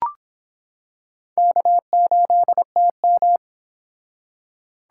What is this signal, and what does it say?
Morse code, heard as a keyed tone, about 26 wpm, 700 Hz. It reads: K8TM